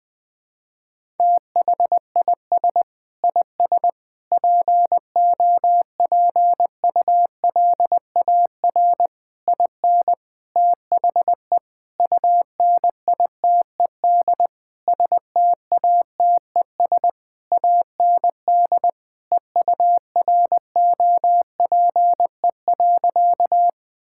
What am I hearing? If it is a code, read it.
Morse code, heard as a keyed tone, about 21 wpm, 700 Hz. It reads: THIS IS POPULAR IN THE UNITED STATES AND EUROPE.
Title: Sentences with 5-Word Phrases - 20wpm